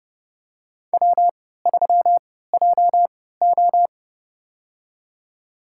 Morse code, heard as a keyed tone, about 30 wpm, spaced as for 17 wpm, 700 Hz. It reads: W3JO
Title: USA Call Signs - 4 Characters - 3x Character Spacing - 30wpm